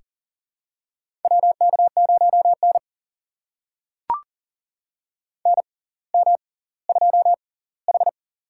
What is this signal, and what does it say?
Morse code, heard as a keyed tone, about 40 wpm, spaced as for 20 wpm, 700 Hz. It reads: WK0N NM2H
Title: USA Call Signs - 4 Characters - 6x Character Spacing - 40wpm